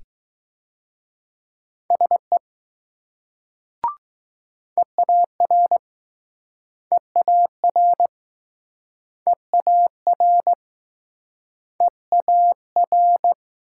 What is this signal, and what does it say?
Morse code, heard as a keyed tone, about 22 wpm, 700 Hz. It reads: SE EAR EAR EAR EAR